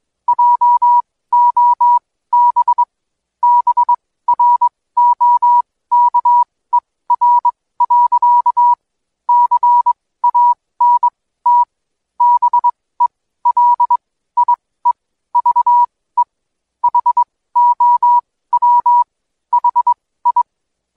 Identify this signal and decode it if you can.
Morse code, heard as a keyed tone, about 21 wpm, spaced as for 15 wpm, 970 Hz. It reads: JOB BROKER. CANT BELIEVE HOW HI
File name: Code Quick / AudioDownld2 / Seg125-DayDrill30-16WPM